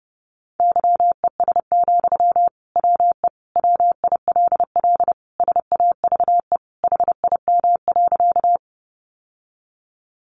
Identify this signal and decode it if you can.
Morse code, heard as a keyed tone, about 30 wpm, 700 Hz. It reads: YES, WE WILL SAVE HIM.
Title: Sentences from Top 1100 Words 30wpm